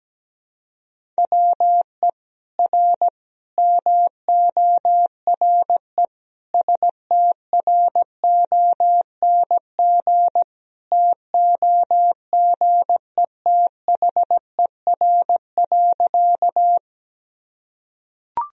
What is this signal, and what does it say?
Morse code, heard as a keyed tone, about 17 wpm, 700 Hz. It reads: WE R MORE STRONG TOGETHER.
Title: Sentences from Top 500 Words and Abbreviations 17wpm